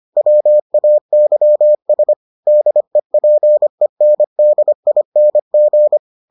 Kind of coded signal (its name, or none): Morse code